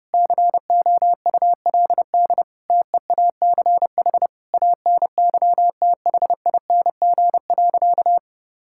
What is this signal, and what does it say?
Morse code, heard as a keyed tone, about 30 wpm, 700 Hz. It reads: COULD TEACH ANYTHING.
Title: Sentences from Top 1000 Words - Review Part 6 - 30wpm